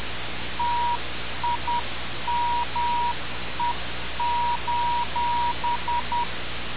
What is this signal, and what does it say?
Morse code, heard as a keyed tone, about 10 wpm, 960 Hz. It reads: TIME:
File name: radiocap / REC z